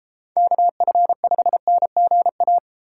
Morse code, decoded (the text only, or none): KF5NGA